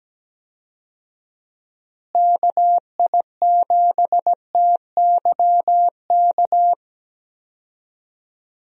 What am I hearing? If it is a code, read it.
Morse code, heard as a keyed tone, about 17 wpm, 700 Hz. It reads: KI7TYK
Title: USA Call Signs - 6 Characters - Repeated 2x - 17wpm